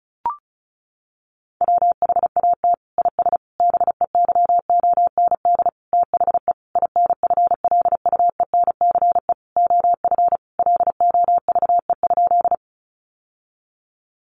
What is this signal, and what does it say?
Morse code, heard as a keyed tone, about 34 wpm, 700 Hz. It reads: WHAT IS BEYOND THE INFLUENCE OF LOVE?
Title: Sentences from Top 1100 Words 35wpm